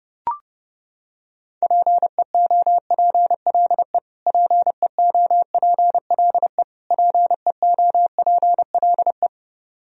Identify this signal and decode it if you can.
Morse code, heard as a keyed tone, about 30 wpm, 700 Hz. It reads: PEOPLE PEOPLE PEOPLE
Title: Top 100 Words Repeated 30wpm